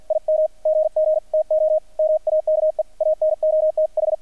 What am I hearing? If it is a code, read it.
Morse code, heard as a keyed tone, about 50 wpm, 620 Hz, with radio noise. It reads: IM GG TO MAKE ANOTH
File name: archived / CRSnet